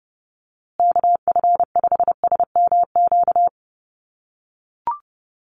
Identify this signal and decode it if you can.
Morse code, heard as a keyed tone, about 30 wpm, 700 Hz. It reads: KF5SMQ
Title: USA Call Signs Repeated 3x 30wpm